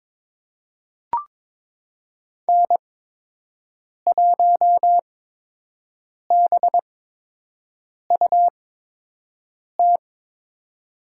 Morse code, heard as a keyed tone, about 22 wpm, spaced as for 6 wpm, 700 Hz. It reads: N1BUT
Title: USA Call Signs - 8x Character Spacing - 22wpm